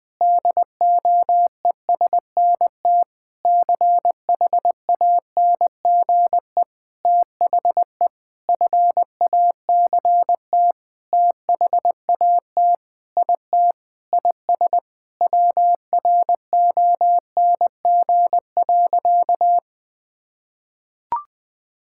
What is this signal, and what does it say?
Morse code, heard as a keyed tone, about 21 wpm, 700 Hz. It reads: DOESNT CHANGE THE FACT THAT IT IS WRONG.
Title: Sentences with 7-Word Phrases - 20wpm